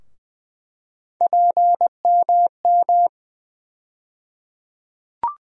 Morse code, heard as a keyed tone, about 20 wpm, 700 Hz. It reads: PMM
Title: USA Call Sign Postfixes - 3 Letters 20wpm